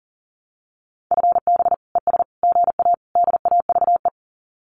Morse code, heard as a keyed tone, about 40 wpm, 700 Hz. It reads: FB ES GA DAVE
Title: Common QSO Phrases Encore 40wpm